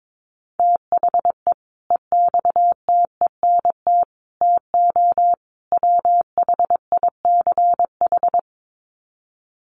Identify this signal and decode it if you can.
Morse code, heard as a keyed tone, about 22 wpm, 700 Hz. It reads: THE EXTENT TO WHICH
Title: Sets of 4 Words 22wpm